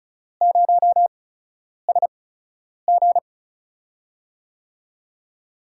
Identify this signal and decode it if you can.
Morse code, heard as a keyed tone, about 36 wpm, spaced as for 10 wpm, 700 Hz. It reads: 0SG
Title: USA Call Signs - 5 Characters - 8x Character Spacing - 35wpm